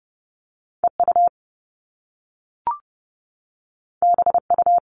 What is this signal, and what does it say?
Morse code, heard as a keyed tone, about 30 wpm, 700 Hz. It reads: EU BU
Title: All Cut Numbers - 2-Digit 30wpm